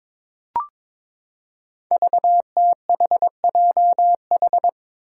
Morse code, heard as a keyed tone, about 23 wpm, 700 Hz. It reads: VTHJH
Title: Five Challenging Letter-Number Combinations 22wpm